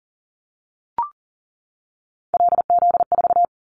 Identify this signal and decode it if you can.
Morse code, heard as a keyed tone, about 40 wpm, 700 Hz. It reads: LZ4
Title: Callsign Prefixes for Top 100 DXCC - 40wpm